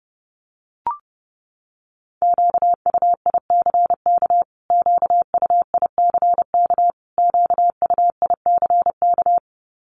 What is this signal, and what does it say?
Morse code, heard as a keyed tone, about 28 wpm, 700 Hz. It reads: QUICK QUICK QUICK
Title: Top 500 Words Repeated 30wpm